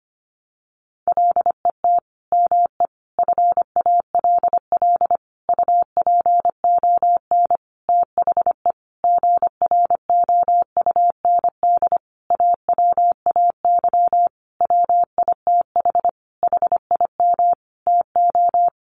Morse code, heard as a keyed tone, about 25 wpm, 700 Hz. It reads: LET ME FALL UPON THE GROUND AWAY WITH HIM TO
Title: Sentences from Top 1100 Words 25wpm